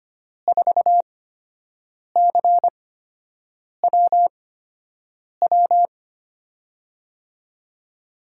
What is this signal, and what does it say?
Morse code, heard as a keyed tone, about 25 wpm, spaced as for 7 wpm, 700 Hz. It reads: 4CWW
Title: USA Call Signs - 5 Characters - 8x Character Spacing - 25wpm